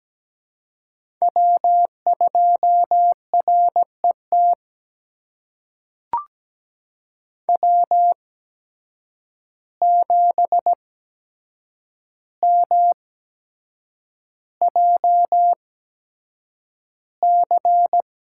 Morse code, heard as a keyed tone, about 17 wpm, spaced as for 7 wpm, 700 Hz. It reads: W2RET W7MJC
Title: USA Call Signs - 5 Characters - 8x Character Spacing - 17wpm